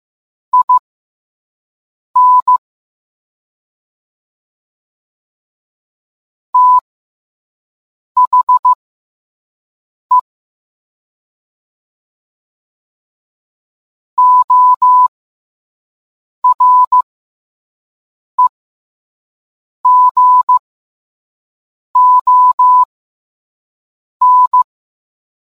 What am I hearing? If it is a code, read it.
Morse code, heard as a keyed tone, about 15 wpm, spaced as for 5 wpm, 1000 Hz. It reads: IN THE OREGON